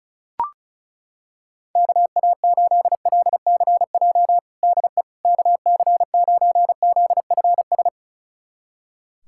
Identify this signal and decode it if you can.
Morse code, heard as a keyed tone, about 34 wpm, 700 Hz. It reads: KA8LCJ DE KC9ZFS
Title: USA Callsign DE Callsign - 6 Characters - 35wpm